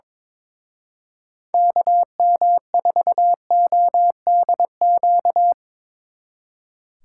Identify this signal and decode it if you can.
Morse code, heard as a keyed tone, about 25 wpm, 700 Hz. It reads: KM4ODQ
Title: Speed Racing - USA 2x3 Call Signs 22wpm